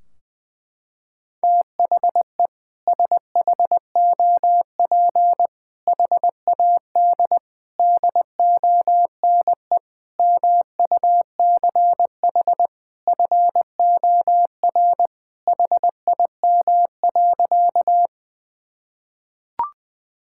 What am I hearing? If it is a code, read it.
Morse code, heard as a keyed tone, about 21 wpm, 700 Hz. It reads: THE SHOP HAD DONE MUCH FOR HIM.
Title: Sentences from Top 1600 Words 20wpm